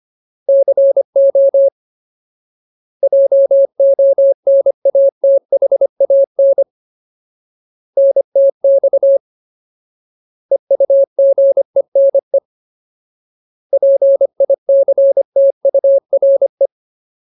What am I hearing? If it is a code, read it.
Morse code, heard as a keyed tone, about 25 wpm, 550 Hz. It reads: CO JONATHAN NTX EUGENE PICTURE
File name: Ultimate Word List / 25WPM Ultimate